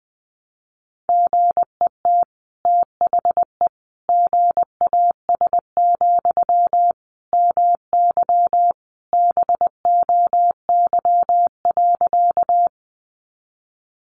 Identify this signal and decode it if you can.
Morse code, heard as a keyed tone, about 20 wpm, 700 Hz. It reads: GET THE GAS, MY BOY.
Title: Sentences from Top 1100 Words 20wpm